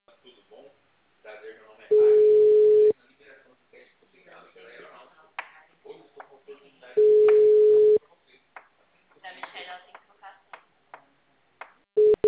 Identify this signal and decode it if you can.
Ringback tone (Europe, 425 Hz), with speech.
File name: out- -2012- - - .252